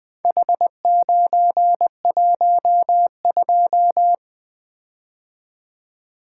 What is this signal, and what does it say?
Morse code, heard as a keyed tone, about 21 wpm, 700 Hz. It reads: H912
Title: Four Letter-Number Combinations - 20wpm